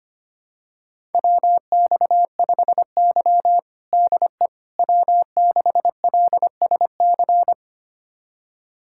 Morse code, heard as a keyed tone, about 25 wpm, 700 Hz. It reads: WX5Y DE W6LSC
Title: USA Callsign DE Callsign 25wpm